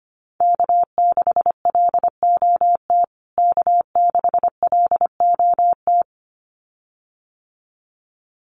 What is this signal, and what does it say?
Morse code, heard as a keyed tone, about 25 wpm, 700 Hz. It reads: K6LOT K6LOT
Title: USA Call Signs Repeated 2x 25wpm